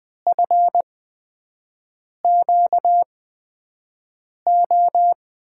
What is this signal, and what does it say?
Morse code, heard as a keyed tone, about 20 wpm, spaced as for 5 wpm, 700 Hz. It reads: FQO
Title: USA Call Signs - 6 Characters - 8x Character Spacing - 20wpm